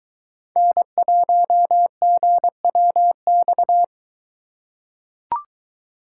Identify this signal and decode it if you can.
Morse code, heard as a keyed tone, about 22 wpm, 700 Hz. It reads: N1GWX